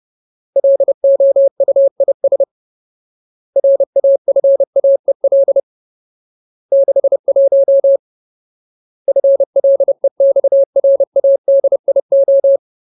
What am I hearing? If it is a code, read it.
Morse code, heard as a keyed tone, about 30 wpm, 550 Hz. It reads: LOUIS RAFAEL 61 FLEXRADIO
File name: Ultimate Word List / 30WPM Ultimate